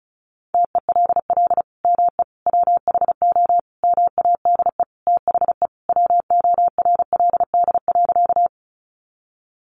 Morse code, heard as a keyed tone, about 34 wpm, 700 Hz. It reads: TELL ME WHO MADE THE WORLD.
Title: Sentences from Top 200 Words 35wpm